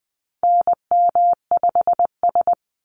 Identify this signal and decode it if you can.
Morse code, heard as a keyed tone, about 20 wpm, 700 Hz. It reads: NM5S
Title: NA SOTA Chaser Callsigns 20wpm